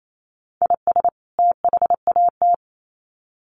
Morse code, heard as a keyed tone, about 28 wpm, 700 Hz. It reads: IS THAT
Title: Sets of 2 Words 28wpm